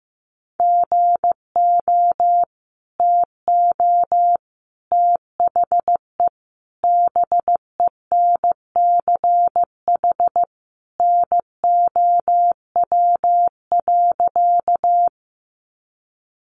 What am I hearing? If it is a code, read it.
Morse code, heard as a keyed tone, about 15 wpm, 700 Hz. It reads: GO TO THE BENCH NOW.
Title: Sentences from Top 1400 Words 15wpm